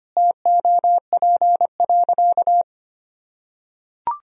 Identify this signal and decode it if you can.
Morse code, heard as a keyed tone, about 25 wpm, 700 Hz. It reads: TOP.
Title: Sentences from Top 1000 Words - Review Part 2 - 25wpm